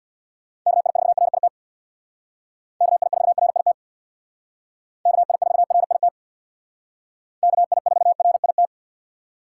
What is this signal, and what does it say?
Morse code, heard as a keyed tone, about 62 wpm, 700 Hz. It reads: KI4RIT KI4RIT KI4RIT KI4RIT
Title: Speed Racing - USA 2x3 Call Signs 50wpm